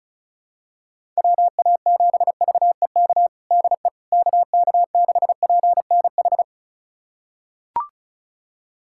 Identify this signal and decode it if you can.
Morse code, heard as a keyed tone, about 36 wpm, 700 Hz. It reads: WA7VEK DE KK6PNH
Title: USA Callsign DE Callsign - 6 Characters - 35wpm